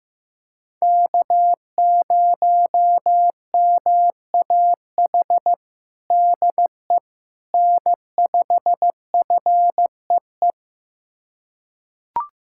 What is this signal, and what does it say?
Morse code, heard as a keyed tone, about 15 wpm, 700 Hz. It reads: K0MAH DE N5FEE
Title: USA Callsign DE Callsign 15wpm